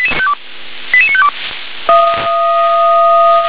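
Five-tone selective call (ZVEI), heard as quick runs of five tones, tones of about 70 ms. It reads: 88952
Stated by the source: Received on SDR as ZVEI five-tone selective call.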